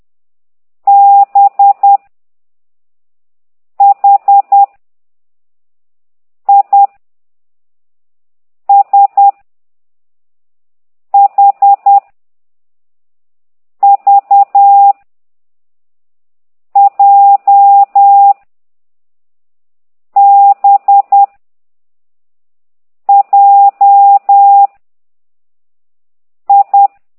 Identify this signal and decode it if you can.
Morse code, heard as a keyed tone, about 10 wpm, 800 Hz. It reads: B H I S H V J B J I
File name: ct2idw / exerc6